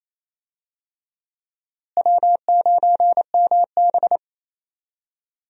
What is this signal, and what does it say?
Morse code, heard as a keyed tone, about 28 wpm, 700 Hz. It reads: W9MB
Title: USA Call Signs - 4 Characters - Repeated 3x - 28wpm